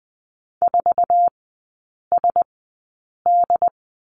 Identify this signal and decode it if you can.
Morse code, heard as a keyed tone, about 20 wpm, spaced as for 8 wpm, 700 Hz. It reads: 4SD
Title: 107 Lesson 18e - Letter m - Three Characters 20wpm